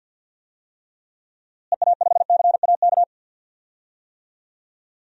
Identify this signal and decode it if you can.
Morse code, heard as a keyed tone, about 47 wpm, 700 Hz. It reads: EA5CAK
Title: DXCC-100 Call Signs - 6 Characters - Repeated 3x - 50wpm